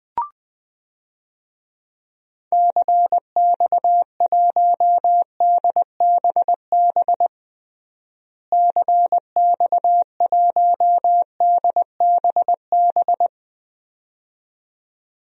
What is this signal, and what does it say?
Morse code, heard as a keyed tone, about 20 wpm, 700 Hz. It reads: CX1DBB CX1DBB
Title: DXCC-100 Call Signs - 6 Characters - Repeated 2x - 20wpm